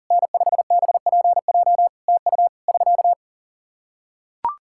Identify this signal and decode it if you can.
Morse code, heard as a keyed tone, about 42 wpm, 700 Hz. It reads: N5BPJ TU <SK>
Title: Practice QSO - Exchange 4 - Ending - 40wpm